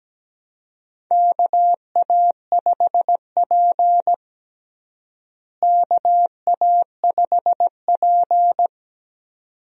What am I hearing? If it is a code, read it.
Morse code, heard as a keyed tone, about 17 wpm, 700 Hz. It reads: KA5P KA5P
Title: USA Call Signs - 4 Characters - Repeated 2x - 17wpm